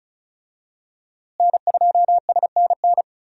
Morse code, heard as a keyed tone, about 34 wpm, 700 Hz. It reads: N2SNN
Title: USA Call Signs - 5 Characters - Repeated 3x - 35wpm